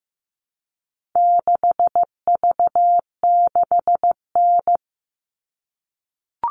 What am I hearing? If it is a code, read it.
Morse code, heard as a keyed tone, about 15 wpm, 700 Hz. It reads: 6V6N
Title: All Cut Numbers - 4-Digit 15wpm